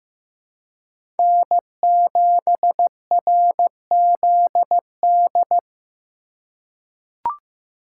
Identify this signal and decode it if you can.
Morse code, heard as a keyed tone, about 15 wpm, 700 Hz. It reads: N7RZD
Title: USA Call Signs - 5 Characters - 4x Character Spacing - 15wpm